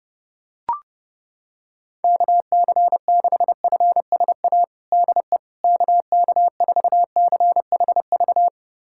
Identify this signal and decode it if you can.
Morse code, heard as a keyed tone, about 32 wpm, 700 Hz. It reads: KC6FSA DE KK4CHV
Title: USA Callsign DE Callsign 30wpm